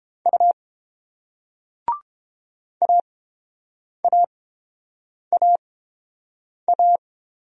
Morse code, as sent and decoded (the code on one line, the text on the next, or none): ..- / .- .- .- .-
U AAAA